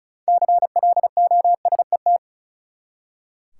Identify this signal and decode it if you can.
Morse code, heard as a keyed tone, about 36 wpm, 700 Hz. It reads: CLOSET
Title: Six Letter Words - 35wpm